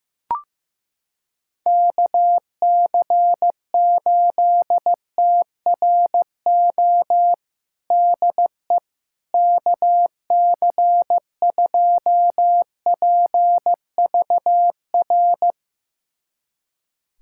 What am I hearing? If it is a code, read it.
Morse code, heard as a keyed tone, about 15 wpm, 700 Hz. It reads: KC8TRO DE KC2PVR
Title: USA Callsign DE Callsign - 6 Characters - 15wpm